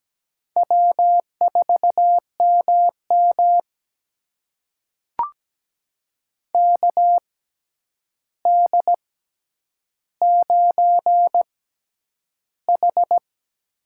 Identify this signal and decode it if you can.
Morse code, heard as a keyed tone, about 17 wpm, spaced as for 9 wpm, 700 Hz. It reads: W4MM KD9H
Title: USA Call Signs - 4 Characters - 6x Character Spacing - 17wpm